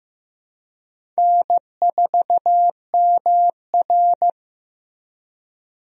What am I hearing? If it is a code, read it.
Morse code, heard as a keyed tone, about 16 wpm, 700 Hz. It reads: N4MR
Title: USA Call Signs - 4 Characters - Repeated 3x - 15wpm